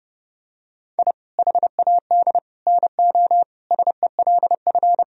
Morse code, heard as a keyed tone, about 32 wpm, 700 Hz. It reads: I HAD NO SELF
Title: Sentences from Top 1000 Words - Review Part 1 - 30wpm